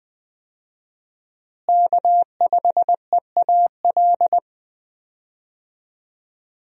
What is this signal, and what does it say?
Morse code, heard as a keyed tone, about 21 wpm, 700 Hz. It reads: K5EAL
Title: USA Call Signs - 5 Characters - Repeated 3x - 20wpm